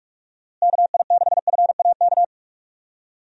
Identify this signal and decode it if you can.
Morse code, heard as a keyed tone, about 46 wpm, 700 Hz. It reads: KI6FAK
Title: USA Call Signs - 6 Characters - Repeated 3x - 45wpm